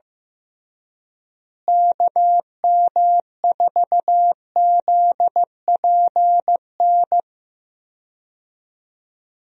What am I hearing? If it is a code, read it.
Morse code, heard as a keyed tone, about 15 wpm, 700 Hz. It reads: KM4ZPN
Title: USA Call Signs - 6 Characters - Repeated 2x - 15wpm